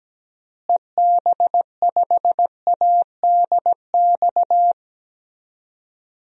Morse code, heard as a keyed tone, about 17 wpm, 700 Hz. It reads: EB5ADX
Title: DXCC-100 Call Signs - 6 Characters - Repeated 2x - 17wpm